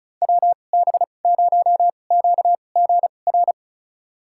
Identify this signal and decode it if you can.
Morse code, heard as a keyed tone, about 36 wpm, spaced as for 25 wpm, 700 Hz. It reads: WB0QGR
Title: USA Call Signs - 6 Characters - 2x Character Spacing - 35wpm